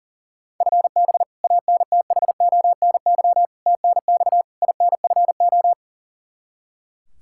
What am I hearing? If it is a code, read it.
Morse code, heard as a keyed tone, about 37 wpm, 700 Hz. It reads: FB ANTHONY TNX INFO
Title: K1USN SST QSO Practice 40wpm